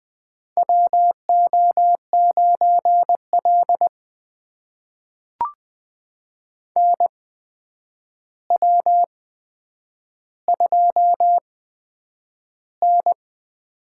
Morse code, heard as a keyed tone, about 20 wpm, spaced as for 9 wpm, 700 Hz. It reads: WO9L NW2N
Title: USA Call Signs - 4 Characters - 8x Character Spacing - 20wpm